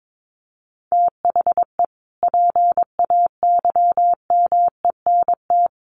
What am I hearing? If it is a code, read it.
Morse code, heard as a keyed tone, about 22 wpm, 700 Hz. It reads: THE PAYMENT
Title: Sentences from Top 1200 Words 22wpm